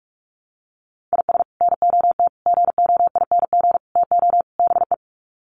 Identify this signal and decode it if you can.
Morse code, heard as a keyed tone, about 44 wpm, 700 Hz. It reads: IS NOT GOING TO BE
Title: Sets of 5 Words 45wpm